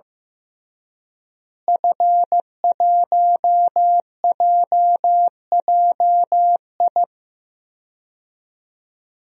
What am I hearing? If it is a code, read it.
Morse code, heard as a keyed tone, about 16 wpm, 700 Hz. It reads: F1JJI
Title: DXCC-100 Call Signs - 5 Characters - Repeated 3x - 15wpm